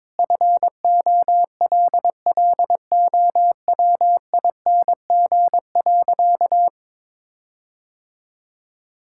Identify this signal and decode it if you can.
Morse code, heard as a keyed tone, about 22 wpm, 700 Hz. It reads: FOLLOWING.
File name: Sentences with 9-Word Phrases - 22wpm